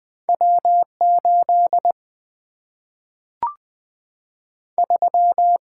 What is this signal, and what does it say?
Morse code, heard as a keyed tone, about 20 wpm, 700 Hz. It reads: W8 3
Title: Single-Letter Single-Number 20wpm